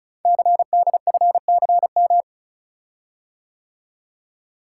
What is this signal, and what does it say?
Morse code, heard as a keyed tone, about 36 wpm, 700 Hz. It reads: CDFCM
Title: Code Groups - 5 Letters - 35wpm